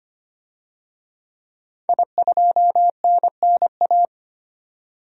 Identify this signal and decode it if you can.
Morse code, heard as a keyed tone, about 25 wpm, 700 Hz. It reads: I2NNA
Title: DXCC-100 Call Signs - 5 Characters - 25wpm